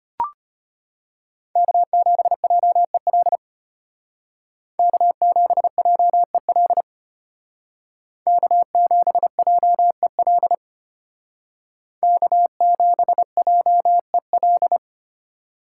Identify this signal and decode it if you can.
Morse code, heard as a keyed tone, about 32 wpm, 700 Hz. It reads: K7JEL K7JEL K7JEL K7JEL